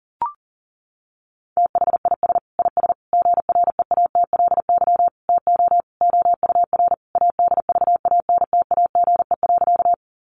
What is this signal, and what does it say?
Morse code, heard as a keyed tone, about 37 wpm, 700 Hz. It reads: THIS IS GREATLY TO OUR ADVANTAGE.
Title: Sentences from Top 1400 Words 40wpm